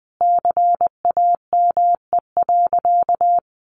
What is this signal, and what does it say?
Morse code, heard as a keyed tone, about 19 wpm, 700 Hz. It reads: CAME.
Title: Sentences from Top 1500 Words 20wpm